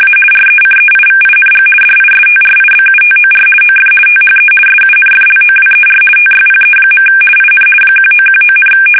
RTTY, about 50 baud. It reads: A RECEIVING DISTRIBUTOR OF THE SUNFLOWER TYPE. IN 1919,
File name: fsk850-50B / teletype-corp